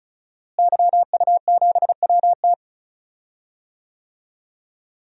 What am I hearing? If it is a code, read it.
Morse code, heard as a keyed tone, about 35 wpm, 700 Hz. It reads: YU7WT
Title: DXCC-100 Call Signs - 5 Characters - Repeated 2x - 35wpm